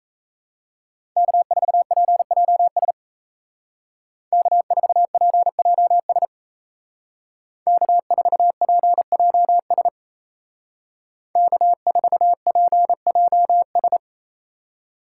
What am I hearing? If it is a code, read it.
Morse code, heard as a keyed tone, about 33 wpm, 700 Hz. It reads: K4PJS K4PJS K4PJS K4PJS
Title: Speed Racing - USA 1x3 Call Signs 28wpm